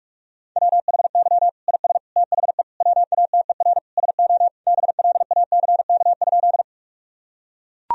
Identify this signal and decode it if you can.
Morse code, heard as a keyed tone, about 41 wpm, 700 Hz. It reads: WHY IS THE WATER SO BLACK?
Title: Sentences from Top 1000 Words - Review Part 2 - 45wpm